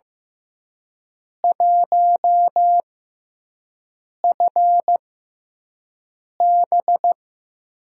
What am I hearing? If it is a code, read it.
Morse code, heard as a keyed tone, about 15 wpm, spaced as for 5 wpm, 700 Hz. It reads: 1FB
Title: USA Call Signs - 5 Characters - 6x Character Spacing - 15wpm